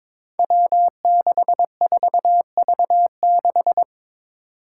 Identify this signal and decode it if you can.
Morse code, heard as a keyed tone, about 22 wpm, 700 Hz. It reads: W64V6
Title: Five Challenging Letter-Number Combinations 22wpm